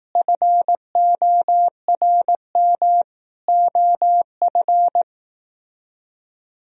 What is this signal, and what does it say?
Morse code, heard as a keyed tone, about 18 wpm, 700 Hz. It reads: FORM OF
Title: Speed-Racing - Sets of 4 Words 15wpm